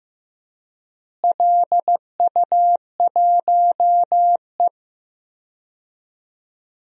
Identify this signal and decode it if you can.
Morse code, heard as a keyed tone, about 15 wpm, 700 Hz. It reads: LU1E
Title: DXCC-100 Call Signs - 4 Characters - Repeated 3x - 15wpm